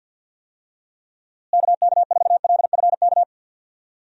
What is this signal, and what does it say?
Morse code, heard as a keyed tone, about 53 wpm, 700 Hz. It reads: KK4LFK
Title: USA Call Signs - 6 Characters - Repeated 3x - 50wpm